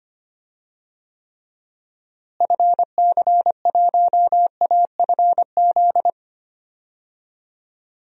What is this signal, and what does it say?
Morse code, heard as a keyed tone, about 25 wpm, 700 Hz. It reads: FC1AFZ
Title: DXCC-100 Call Signs - 6 Characters - Repeated 2x - 25wpm